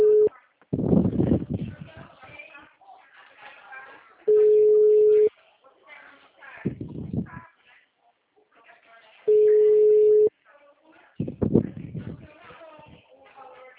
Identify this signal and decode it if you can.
Ringback tone (Europe, 425 Hz), with speech.